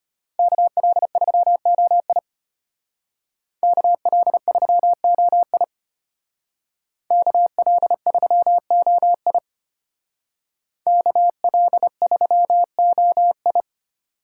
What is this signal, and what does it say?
Morse code, heard as a keyed tone, about 33 wpm, 700 Hz. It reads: KL3OI KL3OI KL3OI KL3OI